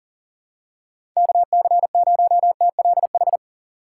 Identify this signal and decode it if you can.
Morse code, heard as a keyed tone, about 40 wpm, 700 Hz. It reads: KC0TLH
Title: USA Call Signs - 6 Characters - 6x Character Spacing - 40wpm